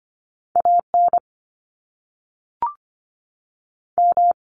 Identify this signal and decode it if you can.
Morse code, heard as a keyed tone, about 25 wpm, 700 Hz. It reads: AN M
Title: Top Two Letter Combinations in Words 25wpm